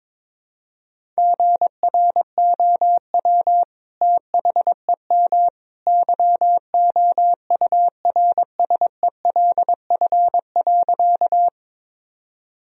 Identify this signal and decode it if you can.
Morse code, heard as a keyed tone, about 21 wpm, 700 Hz. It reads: GROW THEM YOURSELF.
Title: Sentences from Top 1000 Words - Review Part 1 - 22wpm